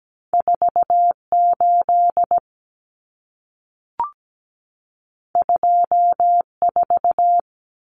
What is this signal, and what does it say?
Morse code, heard as a keyed tone, about 17 wpm, 700 Hz. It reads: 48 24
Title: Numbers - 2-Digit 17wpm